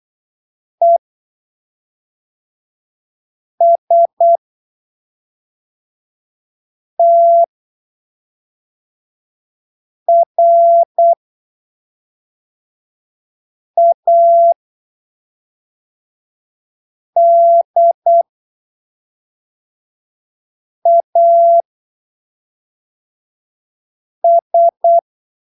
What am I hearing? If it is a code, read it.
Morse code, heard as a keyed tone, about 8 wpm, spaced as for 3 wpm, 680 Hz. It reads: ESTRADAS
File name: arquivos / texto / texto 19 08ppm